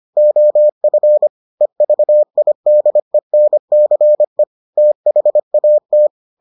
Morse code, heard as a keyed tone, about 25 wpm, 600 Hz. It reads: OF EVIDENCE THAT